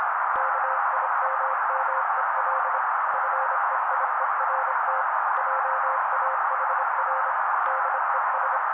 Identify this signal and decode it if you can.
Morse code, heard as a keyed tone, about 26 wpm, 530 Hz. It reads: KIMMEL FEIERT JAHR DES
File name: daily / 25wpm noise